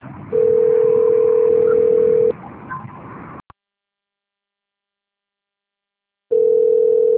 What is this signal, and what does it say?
Ringback tone (North America), 440 + 480 Hz.